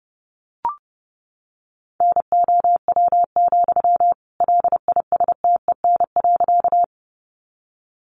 Morse code, heard as a keyed tone, about 30 wpm, 700 Hz. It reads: NOW, LISTEN.
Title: Sentences from Top 400 Words 30wpm